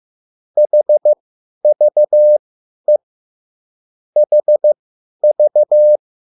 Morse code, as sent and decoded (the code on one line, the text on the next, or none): .... ...- . / .... ...-
HVE HV